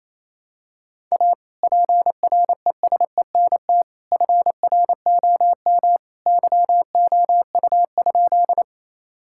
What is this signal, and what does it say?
Morse code, heard as a keyed tone, about 28 wpm, 700 Hz. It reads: A PRESENT FROM YOU?
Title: Sentences from Top 1000 Words - Review Part 5 - 28wpm